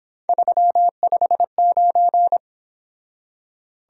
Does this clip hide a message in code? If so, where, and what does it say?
Yes: Morse code from 0.3 to 2.4 s, heard as a keyed tone, about 26 wpm, 700 Hz. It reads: 359